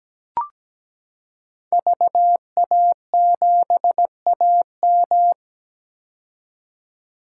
Morse code, heard as a keyed tone, about 17 wpm, 700 Hz. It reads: VA7AM
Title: K1USN SST QSO Practice - 17wpm